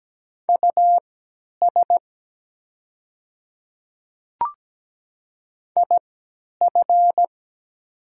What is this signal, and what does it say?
Morse code, heard as a keyed tone, about 17 wpm, spaced as for 10 wpm, 690 Hz. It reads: US IF
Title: 3x Character Spacing - Top 200 Words - 17wpm